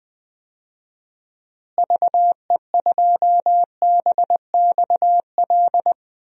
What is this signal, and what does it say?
Morse code, heard as a keyed tone, about 21 wpm, 700 Hz. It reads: VE2BXL
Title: DXCC-100 Call Signs - 6 Characters - Repeated 2x - 20wpm